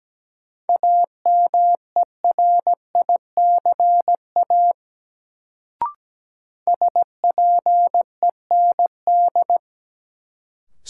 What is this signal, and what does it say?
Morse code, heard as a keyed tone, about 17 wpm, 700 Hz. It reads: AMERICA SPEND
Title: Sets of 5 Words - New Word Drill - 17wpm